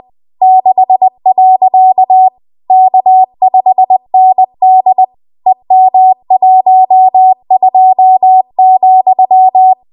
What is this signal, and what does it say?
Morse code, heard as a keyed tone, about 20 wpm, 750 Hz. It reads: 6. K5ND EM12,
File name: Archive / 20 WPM / 20WPM